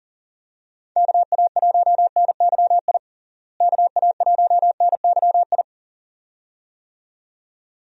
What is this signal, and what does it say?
Morse code, heard as a keyed tone, about 37 wpm, 700 Hz. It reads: KA1NYI KA1NYI
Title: USA Call Signs - 6 Characters - Repeated 2x - 40wpm